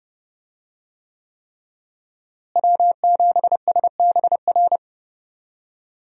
Morse code, heard as a keyed tone, about 30 wpm, 700 Hz. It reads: W7SBR
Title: USA Call Signs - 5 Characters - Repeated 3x - 30wpm